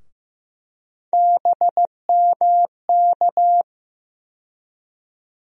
Morse code, heard as a keyed tone, about 15 wpm, 700 Hz. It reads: BMK
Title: USA Call Sign Postfixes - 3 Letters 15wpm